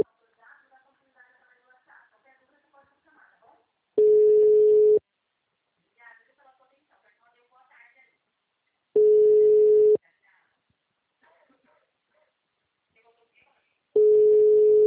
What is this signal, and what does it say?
Ringback tone (Europe, 425 Hz), with speech.